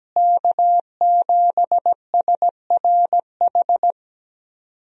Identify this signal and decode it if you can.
Morse code, heard as a keyed tone, about 17 wpm, 700 Hz. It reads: K7SRH
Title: USA Call Signs - 5 Characters - Repeated 3x - 17wpm